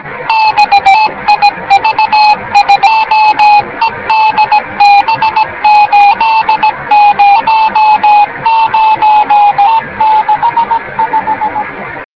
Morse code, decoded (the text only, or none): none